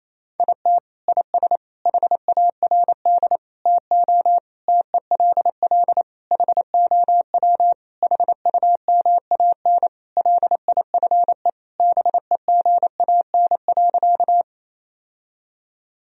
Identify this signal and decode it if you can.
Morse code, heard as a keyed tone, about 28 wpm, 700 Hz. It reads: IT IS HARD TO TELL HOW HUMAN LIFE BEGAN.
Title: Sentences from Top 1000 Words - Review Part 3 - 28wpm